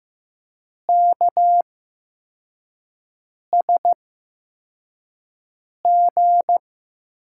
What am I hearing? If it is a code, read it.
Morse code, heard as a keyed tone, about 15 wpm, spaced as for 4 wpm, 700 Hz. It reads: KSG
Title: USA Call Signs - 8x Character Spacing - 15wpm